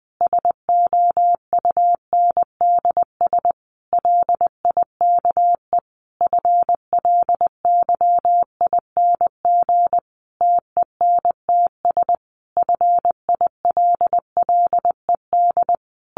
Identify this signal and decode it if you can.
Morse code, heard as a keyed tone, about 20 wpm, 700 Hz. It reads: SOUNDS LIKE FLYING TENTS FILLED
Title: A Sound Of Thunder 20wpm